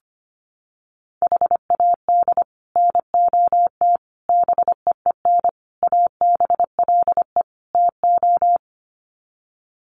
Morse code, heard as a keyed tone, about 25 wpm, 700 Hz. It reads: HAD NOT BEEN ABLE TO
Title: Sets of 5 Words Encore 25wpm